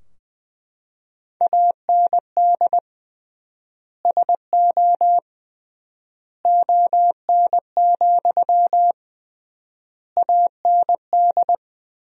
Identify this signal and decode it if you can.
Morse code, heard as a keyed tone, about 20 wpm, 700 Hz. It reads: AND SO ON, AND
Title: 3x Word-Spacing - Sentences from Top 100 Words - 20wpm